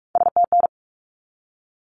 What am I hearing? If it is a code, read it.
Morse code, heard as a keyed tone, about 44 wpm, 700 Hz. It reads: STN
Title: Abbreviations 45wpm